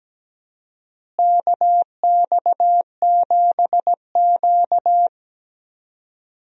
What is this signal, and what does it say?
Morse code, heard as a keyed tone, about 17 wpm, 700 Hz. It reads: KX7Q
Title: USA Call Signs - 4 Characters - Repeated 3x - 17wpm